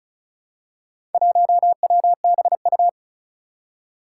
Morse code, heard as a keyed tone, about 36 wpm, 700 Hz. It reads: 1WBU
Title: Four Challenging Letter-Number Combinations 35wpm